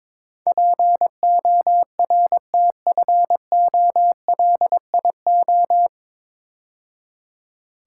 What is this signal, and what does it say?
Morse code, heard as a keyed tone, about 23 wpm, 700 Hz. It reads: PORTFOLIO
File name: 9 Letter Words - First 3 Letters Given - 22wpm